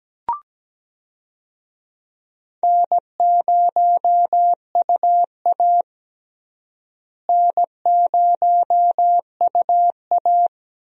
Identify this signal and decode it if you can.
Morse code, heard as a keyed tone, about 17 wpm, 700 Hz. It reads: N0UA N0UA
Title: USA Call Signs - 4 Characters - Repeated 3x - 17wpm